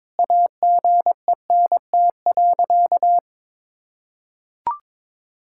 Morse code, heard as a keyed tone, about 22 wpm, 700 Hz. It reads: AGENT.
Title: Sentences from Top 1600 Words 22wpm